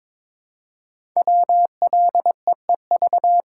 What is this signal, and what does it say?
Morse code, heard as a keyed tone, about 22 wpm, 700 Hz. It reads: WLEEV
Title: Code Groups - 5 Letters - 22wpm